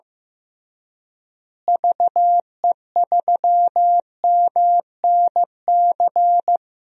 Morse code, heard as a keyed tone, about 16 wpm, 700 Hz. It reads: VE3MNC
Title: DXCC-100 Call Signs - 6 Characters - Repeated 3x - 15wpm